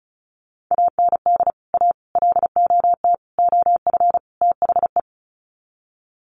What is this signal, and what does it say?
Morse code, heard as a keyed tone, about 34 wpm, 700 Hz. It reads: AND A LOT OF THE
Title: Sets of 5 Words Encore 35wpm